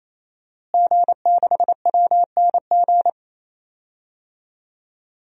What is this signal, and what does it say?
Morse code, heard as a keyed tone, about 28 wpm, 700 Hz. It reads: G6WNG
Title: DXCC-100 Call Signs - 5 Characters - Repeated 3x - 28wpm